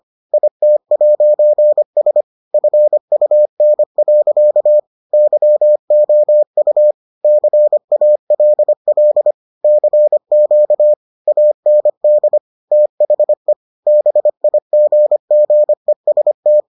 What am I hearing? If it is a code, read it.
Morse code, heard as a keyed tone, about 25 wpm, 600 Hz. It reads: IT'S FUN. YOU CALL CQ AND THE BIGGEST